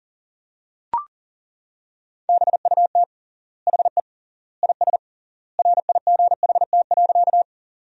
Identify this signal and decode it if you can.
Morse code, heard as a keyed tone, about 40 wpm, 700 Hz. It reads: BUT HE IS RIGHT.
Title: 3x Word-Spacing - Sentences from Top 200 Words - 40wpm